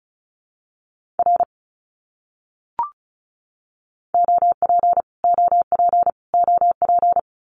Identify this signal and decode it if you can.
Morse code, heard as a keyed tone, about 34 wpm, 700 Hz. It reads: R OP OP OP
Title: Abbreviations Repeated 3x - 35wpm